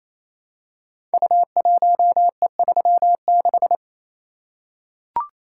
Morse code, heard as a keyed tone, about 27 wpm, 700 Hz. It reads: U1E36
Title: Code Groups - 5 Letter:Numbers - 28wpm